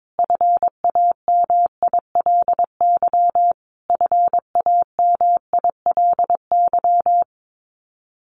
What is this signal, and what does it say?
Morse code, heard as a keyed tone, about 22 wpm, 700 Hz. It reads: FAMILY FAMILY
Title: Top 500 Words Repeated 22wpm